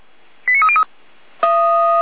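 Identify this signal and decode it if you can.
Five-tone selective call (ZVEI), heard as quick runs of five tones, tones of about 70 ms. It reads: 89303
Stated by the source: Five-tone ZVEI selective call received by SDR, feuerwehr-orlamuende.org fire brigade alerting archive.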